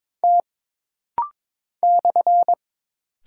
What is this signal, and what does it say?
Morse code, heard as a keyed tone, about 22 wpm, 700 Hz. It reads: T /
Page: This rapid-fire timing is geared to push your ICR (Instant Character Recognition) skills.